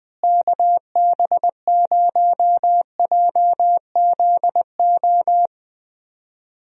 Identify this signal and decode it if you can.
Morse code, heard as a keyed tone, about 20 wpm, 700 Hz. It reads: KB0JZO
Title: USA Call Signs - 6 Characters - Repeated 2x - 20wpm